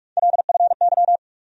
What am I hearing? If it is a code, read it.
Morse code, heard as a keyed tone, about 46 wpm, 700 Hz. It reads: LFY
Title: USA Call Sign Postfixes - 3 Letters 45wpm